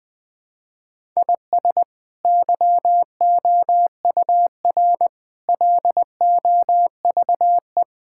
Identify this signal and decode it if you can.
Morse code, heard as a keyed tone, about 20 wpm, 700 Hz. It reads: IS YOUR LOVE
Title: Sentences from Top 1000 Words - Review Part 5 - 20wpm